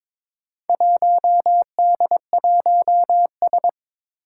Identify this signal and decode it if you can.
Morse code, heard as a keyed tone, about 22 wpm, 700 Hz. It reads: 1D1S
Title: Four Challenging Letter-Number Combinations 22wpm